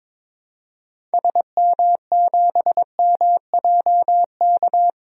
Morse code, heard as a keyed tone, about 22 wpm, 700 Hz. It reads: SM7MJK
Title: DXCC-100 Call Signs - 6 Characters - 22wpm